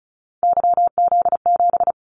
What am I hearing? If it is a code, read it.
Morse code, heard as a keyed tone, about 34 wpm, 700 Hz. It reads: YZ7
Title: Callsign Prefixes for Top 25 DXCC - 35wpm